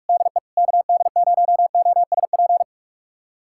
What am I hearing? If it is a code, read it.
Morse code, heard as a keyed tone, about 41 wpm, 700 Hz. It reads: DE KD0OSP
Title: USA Callsign DE Callsign 45wpm